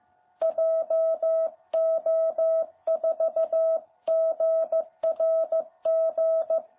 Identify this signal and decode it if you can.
Morse code, heard as a keyed tone, about 14 wpm, 660 Hz. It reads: JO4GRG
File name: jh8cbh / 2021cw